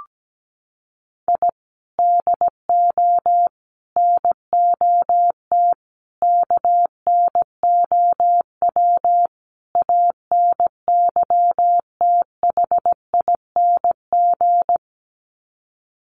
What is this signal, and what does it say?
Morse code, heard as a keyed tone, about 17 wpm, 700 Hz. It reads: I DO NOT KNOW ANYTHING
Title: Sets of 5 Words Encore 17wpm